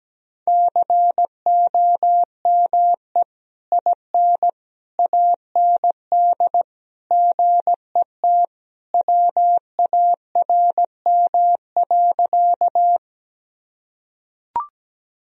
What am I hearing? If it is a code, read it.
Morse code, heard as a keyed tone, about 17 wpm, 700 Hz. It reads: COME IN AND GET WARM.
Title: Sentences from Top 1000 Words - Review Part 2 - 17wpm